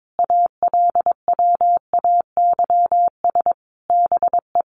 Morse code, heard as a keyed tone, about 22 wpm, 700 Hz. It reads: ALWAYS BE
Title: Sentences from Top 1300 Words 22wpm